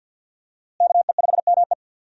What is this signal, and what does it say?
Morse code, heard as a keyed tone, about 50 wpm, 700 Hz. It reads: KE5ME
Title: USA Call Signs - 5 Characters - Repeated 3x - 50wpm